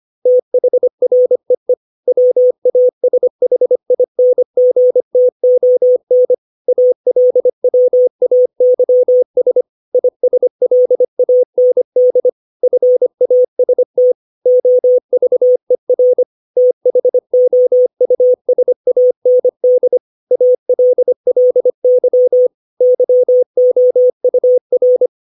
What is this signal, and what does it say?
Morse code, heard as a keyed tone, about 25 wpm, 500 Hz. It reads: THREE WASHINGTON ALWAYS ISLAND FAST OVER THOUSAND ALLY YOUR
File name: 632 words / 632 Words 25WPM